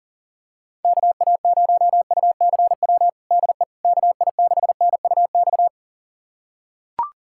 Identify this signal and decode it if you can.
Morse code, heard as a keyed tone, about 40 wpm, 700 Hz. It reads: KA0UCW DE KI6NUX
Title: USA Callsign DE Callsign - 6 Characters - 40wpm